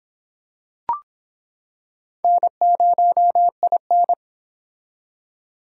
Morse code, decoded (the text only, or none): N0IN